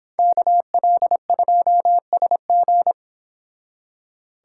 Morse code, heard as a keyed tone, about 30 wpm, 700 Hz. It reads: KL2SG